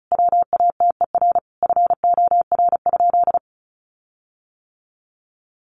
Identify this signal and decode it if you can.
Morse code, heard as a keyed tone, about 34 wpm, 700 Hz. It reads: WATER FOR?
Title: Sentences from Top 1300 Words 35wpm